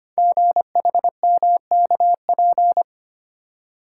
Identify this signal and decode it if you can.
Morse code, heard as a keyed tone, about 25 wpm, 700 Hz. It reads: GHMKP
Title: Code Groups - 5 Letters - 25wpm